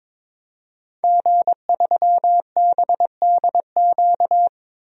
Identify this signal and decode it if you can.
Morse code, heard as a keyed tone, about 21 wpm, 700 Hz. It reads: G3BDQ
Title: DXCC-100 Call Signs - 5 Characters - 22wpm